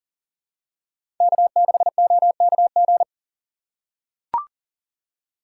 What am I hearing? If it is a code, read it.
Morse code, heard as a keyed tone, about 40 wpm, 700 Hz. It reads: K6OKG
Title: USA Call Signs - 2x Character Spacing - 40wpm